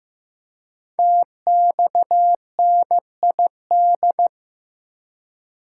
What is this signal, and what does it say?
Morse code, heard as a keyed tone, about 15 wpm, 700 Hz. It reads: TXNID
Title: Code Groups - 5 Letter:Numbers - 15wpm